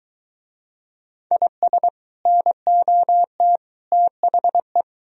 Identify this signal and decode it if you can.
Morse code, heard as a keyed tone, about 23 wpm, 700 Hz. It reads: IS NOT THE